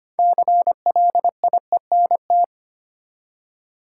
Morse code, heard as a keyed tone, about 25 wpm, 700 Hz. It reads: CLIENT
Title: Six Letter Words - 25wpm